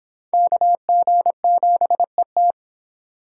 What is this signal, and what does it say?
Morse code, heard as a keyed tone, about 25 wpm, 700 Hz. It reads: KG7ET